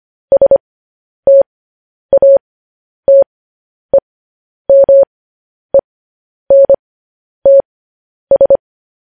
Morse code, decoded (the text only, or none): STATEMENTS